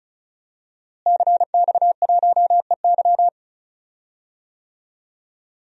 Morse code, heard as a keyed tone, about 36 wpm, 700 Hz. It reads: CX1EY
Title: DXCC-100 Call Signs - 5 Characters - Repeated 3x - 35wpm